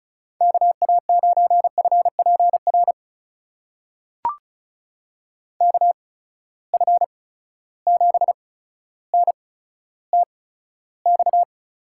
Morse code, heard as a keyed tone, about 36 wpm, spaced as for 15 wpm, 700 Hz. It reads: KA9FPR KF7NTX
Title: USA Call Signs - 6 Characters - 8x Character Spacing - 35wpm